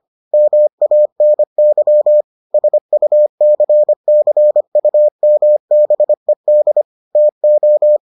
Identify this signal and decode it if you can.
Morse code, heard as a keyed tone, about 25 wpm, 600 Hz. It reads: MANY SUCCUMBED TO